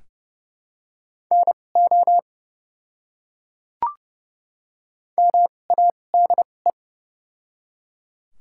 Morse code, heard as a keyed tone, about 30 wpm, spaced as for 22 wpm, 690 Hz. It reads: NO MADE
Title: 2x Character Spacing - Top 300 Words - 30wpm